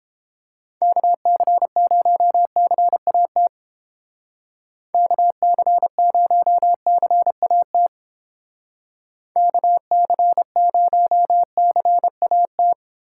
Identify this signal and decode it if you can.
Morse code, heard as a keyed tone, about 30 wpm, 700 Hz. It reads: KC0CAT KC0CAT KC0CAT